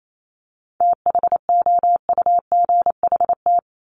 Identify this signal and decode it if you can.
Morse code, heard as a keyed tone, about 28 wpm, 700 Hz. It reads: THOUGHT
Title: Top 500 Words 28wpm